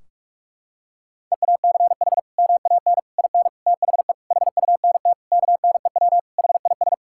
Morse code, heard as a keyed tone, about 47 wpm, 700 Hz. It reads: EACH MAN IN THE HUNT KNEW HIS
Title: Sentences from Top 1600 Words 45wpm